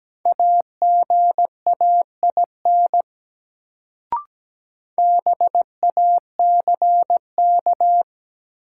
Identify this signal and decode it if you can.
Morse code, heard as a keyed tone, about 17 wpm, 700 Hz. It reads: AGAIN BACK
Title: Sentences from Top 200 Words - New Word Drill - 17wpm